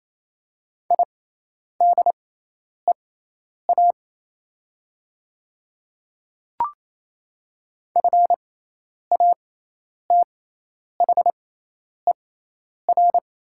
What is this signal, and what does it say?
Morse code, heard as a keyed tone, about 27 wpm, spaced as for 10 wpm, 700 Hz. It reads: IDEA FATHER
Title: 6x Character Spacing - Top 300 Words - 28wpm